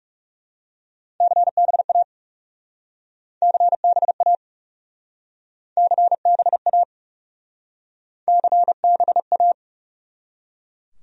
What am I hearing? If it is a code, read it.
Morse code, heard as a keyed tone, about 40 wpm, 700 Hz. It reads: CBA CBA CBA CBA